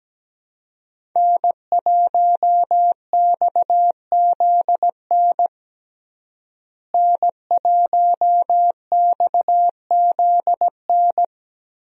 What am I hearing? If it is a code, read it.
Morse code, heard as a keyed tone, about 17 wpm, 700 Hz. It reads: N1XZN N1XZN
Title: USA Call Signs - 5 Characters - Repeated 2x - 17wpm